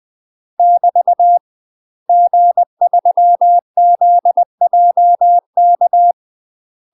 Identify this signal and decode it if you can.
Morse code, heard as a keyed tone, about 20 wpm, 700 Hz. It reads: <BT> G3ZJK
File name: UK MASTER QSOs TRAINING / 20WPM / UK MASTER QSOs 0036